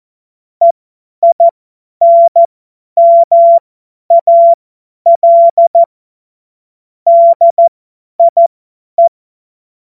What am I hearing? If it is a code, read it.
Morse code, heard as a keyed tone, about 14 wpm, spaced as for 10 wpm, 680 Hz. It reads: EINMAL DIE
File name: Morsekurs / Witze / Witz 23. August 2022 50 F70